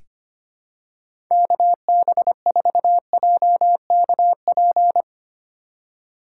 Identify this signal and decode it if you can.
Morse code, heard as a keyed tone, about 25 wpm, 700 Hz. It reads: KB4JKP
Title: USA Call Signs - 4x Character Spacing - 25wpm